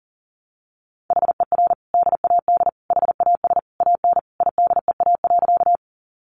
Morse code, heard as a keyed tone, about 40 wpm, 700 Hz. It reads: HER DAD HAS AN IDEA.
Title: Easy Phrases 40wpm